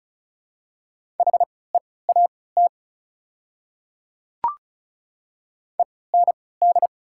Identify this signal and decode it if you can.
Morse code, heard as a keyed tone, about 36 wpm, spaced as for 20 wpm, 690 Hz. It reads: HEAT END
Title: 3x Character Spacing - Top 500 Words - 35wpm